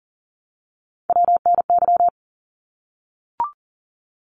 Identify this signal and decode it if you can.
Morse code, heard as a keyed tone, about 41 wpm, 700 Hz. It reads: WNY
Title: ARRL Sections 40wpm